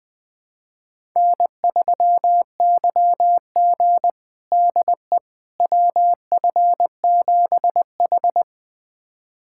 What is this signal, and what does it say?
Morse code, heard as a keyed tone, about 20 wpm, 700 Hz. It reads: N3YG DE WF7H
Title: USA Callsign DE Callsign - 4 Characters - 20wpm